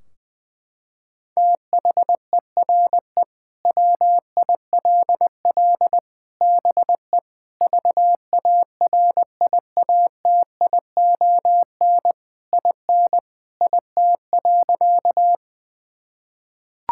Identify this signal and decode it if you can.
Morse code, heard as a keyed tone, about 20 wpm, 700 Hz. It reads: THERE WILL BE VARIATION IN IT.
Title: Sentences from Top 1600 Words 20wpm